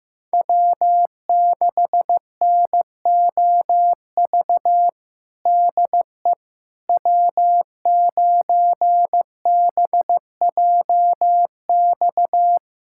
Morse code, heard as a keyed tone, about 15 wpm, 700 Hz. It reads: W6NOV DE W9BJX
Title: USA Callsign DE Callsign - 5 Characters - 15wpm